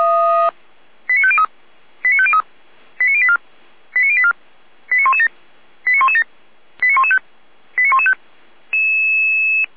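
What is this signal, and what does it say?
Five-tone selective call (ZVEI), heard as quick runs of five tones, tones of about 70 ms. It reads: 89503 89503 89084 89084 89117 89117 89115 89115
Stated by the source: ZVEI five-tone selective call 89092 received by SDR.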